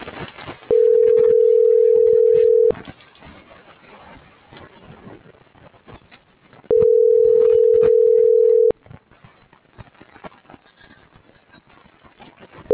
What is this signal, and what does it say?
Ringback tone (North America), 440 + 480 Hz.
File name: CENTRAL TERRESTRE / out claudiacruz - .62201